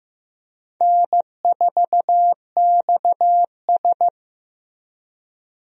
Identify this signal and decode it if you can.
Morse code, heard as a keyed tone, about 16 wpm, 700 Hz. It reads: N4XS
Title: USA Call Signs - 4 Characters - Repeated 2x - 15wpm